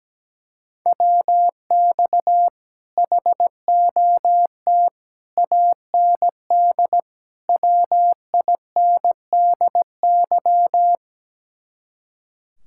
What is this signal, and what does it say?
Morse code, heard as a keyed tone, about 17 wpm, 700 Hz. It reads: WX HOT AND WINDY
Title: Speed-Racing Common QSO Phrases - 17wpm